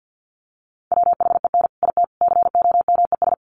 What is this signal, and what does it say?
Morse code, heard as a keyed tone, about 50 wpm, 700 Hz. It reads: WHEN IT COMES
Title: Sets of 3 Words 50wpm